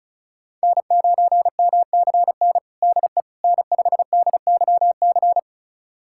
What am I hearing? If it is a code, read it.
Morse code, heard as a keyed tone, about 34 wpm, 700 Hz. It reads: N9MCN DE N5DYC
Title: USA Callsign DE Callsign - 5 Characters - 35wpm